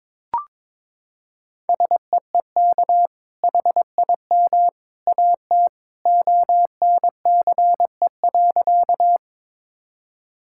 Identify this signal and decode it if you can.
Morse code, heard as a keyed tone, about 22 wpm, 700 Hz. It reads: SEEK HIM AT ONCE.
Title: Sentences from Top 1600 Words 22wpm